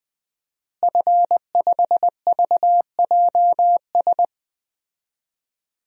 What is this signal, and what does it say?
Morse code, heard as a keyed tone, about 20 wpm, 700 Hz. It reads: F5VJS
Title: DXCC-100 Call Signs - 5 Characters - Repeated 2x - 20wpm